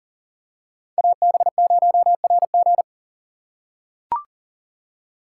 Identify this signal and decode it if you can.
Morse code, heard as a keyed tone, about 40 wpm, 700 Hz. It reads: AB0RG
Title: USA Call Signs - 5 Characters - 2x Character Spacing - 40wpm